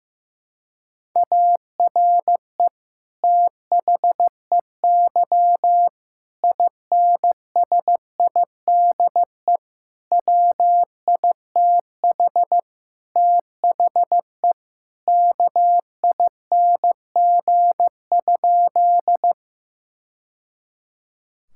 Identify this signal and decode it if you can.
Morse code, heard as a keyed tone, about 15 wpm, 700 Hz. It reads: ARE THEY INSIDE WITH THE KING?
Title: Sentences from Top 1000 Words - Review Part 6 - 15wpm